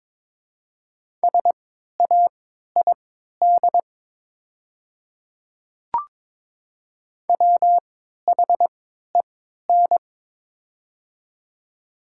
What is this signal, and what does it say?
Morse code, heard as a keyed tone, about 23 wpm, spaced as for 13 wpm, 700 Hz. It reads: SAID WHEN
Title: 3x Character Spacing - Top 100 Words - 22wpm